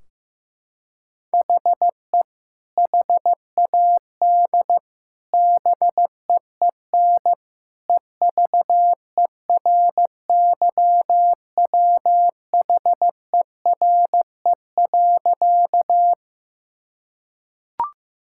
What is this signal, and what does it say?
Morse code, heard as a keyed tone, about 15 wpm, 700 Hz. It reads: HE HAD BEEN EVERYWHERE.
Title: Sentences from Top 1600 Words 15wpm